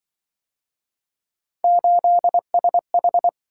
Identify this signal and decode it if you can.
Morse code, heard as a keyed tone, about 26 wpm, 700 Hz. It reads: 8SH